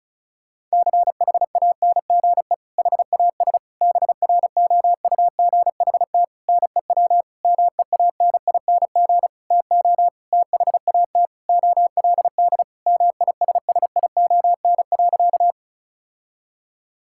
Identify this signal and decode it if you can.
Morse code, heard as a keyed tone, about 36 wpm, 700 Hz. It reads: CHANGE HAS BROUGHT NEW MEANING TO THAT OLD MISSION.
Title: Sentences from Top 1600 Words 35wpm